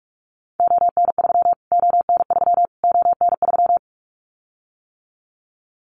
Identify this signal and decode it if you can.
Morse code, heard as a keyed tone, about 44 wpm, 700 Hz. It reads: ON3 ON3 ON3
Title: Callsign Prefixes for Top 25 DXCC - Repeated 3x - 45wpm